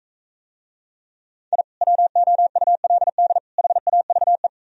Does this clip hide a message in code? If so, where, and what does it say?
Yes: Morse code from 1.5 to 4.5 s, heard as a keyed tone, about 42 wpm, 700 Hz. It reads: I WOULD HAVE